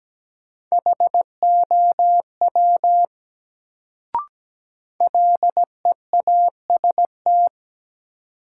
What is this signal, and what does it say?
Morse code, heard as a keyed tone, about 17 wpm, 700 Hz. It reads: HOW LEAST
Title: Sets of 2 Words - New Word Drill - 17wpm